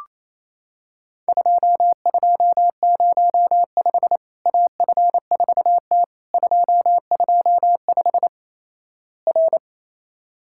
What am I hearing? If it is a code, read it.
Morse code, heard as a keyed tone, about 28 wpm, 700 Hz. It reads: 2205 AF4T 225 R
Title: Worked All Europe DX Contest - QTC Exchange - 28wpm